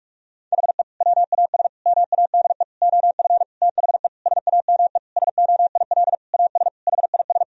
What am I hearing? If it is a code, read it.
Morse code, heard as a keyed tone, about 46 wpm, 700 Hz. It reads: HE WAS MADE OF THE SAME SOIL AS HIS
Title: Sentences from Top 1000 Words - Review Part 4 - 45wpm